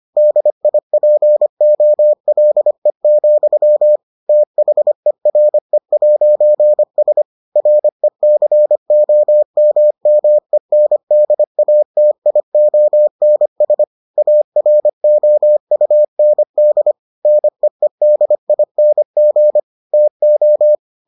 Morse code, heard as a keyed tone, about 25 wpm, 600 Hz. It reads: DIPOLE, THERE'S RECOMMENDATIONS AROUND NEEDING TO